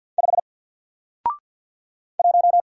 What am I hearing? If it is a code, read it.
Morse code, heard as a keyed tone, about 53 wpm, 700 Hz. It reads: 5 1
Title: Single Letter-Number Alt 50wpm